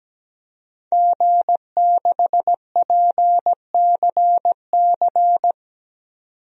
Morse code, heard as a keyed tone, about 17 wpm, 700 Hz. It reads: G6PCC
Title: DXCC-100 Call Signs - 5 Characters - 17wpm